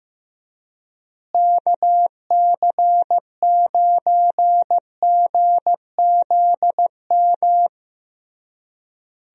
Morse code, heard as a keyed tone, about 15 wpm, 700 Hz. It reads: KC9GZM
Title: USA Call Signs - 6 Characters - Repeated 3x - 15wpm